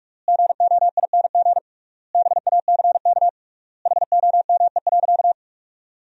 Morse code, heard as a keyed tone, about 42 wpm, 700 Hz. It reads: GOING BACK HOME.
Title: 3x Word-Spacing - Sentences from Top 500 Words - 45wpm